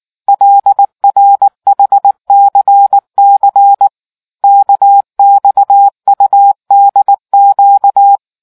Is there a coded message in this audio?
Yes: Morse code from 0.3 to 8.2 s, heard as a keyed tone, about 19 wpm, 800 Hz. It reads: LRHCC KXUDQ